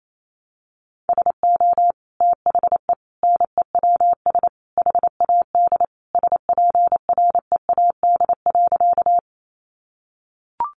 Morse code, heard as a keyed tone, about 28 wpm, 700 Hz. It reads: SO THE NEWS HAD SPREAD.
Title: Sentences from Top 1200 Words 28wpm